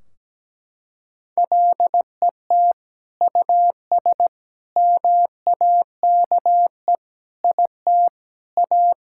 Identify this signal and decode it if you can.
Morse code, heard as a keyed tone, about 17 wpm, 700 Hz. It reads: LET US MAKE IT A
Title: Sentences from Top 1600 Words 17wpm